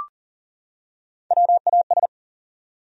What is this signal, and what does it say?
Morse code, heard as a keyed tone, about 40 wpm, 700 Hz. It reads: WAS
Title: Top 100 Words 40wpm